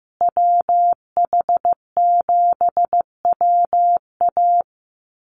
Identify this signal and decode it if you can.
Morse code, heard as a keyed tone, about 15 wpm, 700 Hz. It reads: WH7WA
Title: USA Call Signs Repeated 2x 15wpm